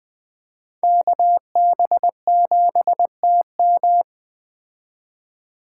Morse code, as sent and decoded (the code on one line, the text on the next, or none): -.- -... --... - --
KB7TM